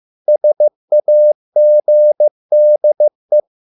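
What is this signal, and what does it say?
Morse code, heard as a keyed tone, about 15 wpm, 600 Hz. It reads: SAGDE
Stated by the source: Morse code at a slow 15 wpm from CW med Gnister.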